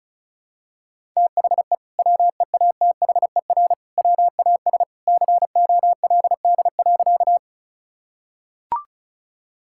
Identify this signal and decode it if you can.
Morse code, heard as a keyed tone, about 36 wpm, 700 Hz. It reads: THE WEATHER WAS COLD.
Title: Sentences from Top 1000 Words - Review Part 5 - 35wpm